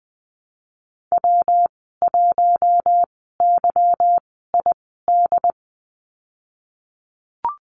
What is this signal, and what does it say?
Morse code, heard as a keyed tone, about 19 wpm, spaced as for 14 wpm, 700 Hz. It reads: W1YID
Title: 179 Lesson 29g - Slash - Call Signs 20wpm